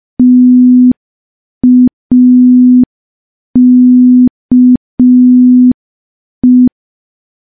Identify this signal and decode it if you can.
Morse code, heard as a keyed tone, about 5 wpm, 250 Hz. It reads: TAKE
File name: morse / morse-training-05WPM